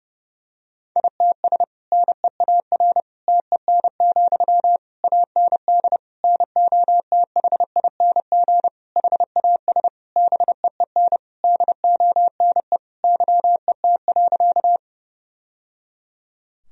Morse code, heard as a keyed tone, about 32 wpm, 700 Hz. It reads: ITS NEAR TEN, AND NOTHING HAS BEEN DONE YET.
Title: Sentences from Top 1000 Words - Review Part 2 - 30wpm